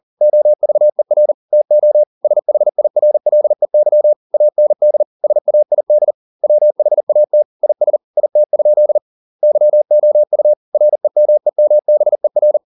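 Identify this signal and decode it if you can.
Morse code, heard as a keyed tone, about 40 wpm, 600 Hz. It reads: OVER TO SHIRLEY AND SAID WHAT IS IT? YOU REMEMBER
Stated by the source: Morse code at 40 wpm, practice text, CW med Gnister.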